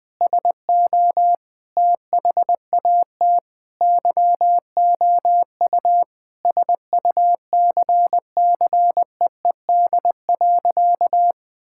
Morse code, heard as a keyed tone, about 20 wpm, 700 Hz. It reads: SO THAT YOU SUCCEED.
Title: Sentences with 2-Word Phrases - 20wpm